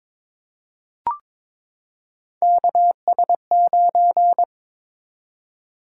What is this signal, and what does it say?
Morse code, heard as a keyed tone, about 22 wpm, 700 Hz. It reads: KS9
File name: practice / 22wpm / Three Letter-Number Combinations - 22wpm